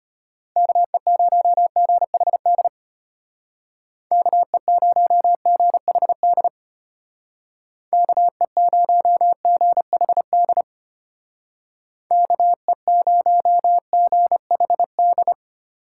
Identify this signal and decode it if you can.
Morse code, heard as a keyed tone, about 33 wpm, 700 Hz. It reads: KE0GHD KE0GHD KE0GHD KE0GHD